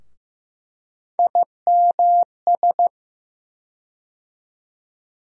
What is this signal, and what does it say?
Morse code, heard as a keyed tone, about 15 wpm, 700 Hz. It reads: IMS
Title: USA Call Sign Postfixes - 3 Letters 15wpm